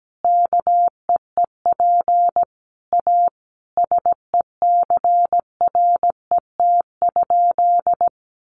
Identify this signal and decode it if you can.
Morse code, heard as a keyed tone, about 17 wpm, 700 Hz. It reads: KEEP A SECRET?
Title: Sentences from Top 1200 Words 17wpm